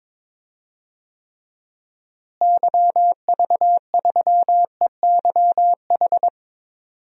Morse code, heard as a keyed tone, about 21 wpm, 700 Hz. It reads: YV3EYH
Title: DXCC-100 Call Signs - 6 Characters - Repeated 2x - 22wpm